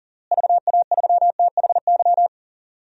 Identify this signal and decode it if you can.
Morse code, heard as a keyed tone, about 40 wpm, 700 Hz. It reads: VA3THY
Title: DXCC-100 Call Signs - 6 Characters - Repeated 3x - 40wpm